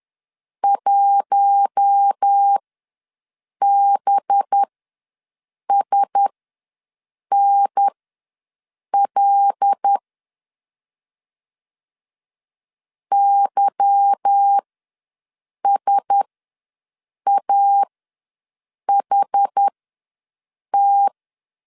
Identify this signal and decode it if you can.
Morse code, heard as a keyed tone, about 10 wpm, spaced as for 6 wpm, 800 Hz. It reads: 1BSNL YSAHT
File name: LEZIONI / 10 SIGNS / 10C DOT ws 10wpm 4213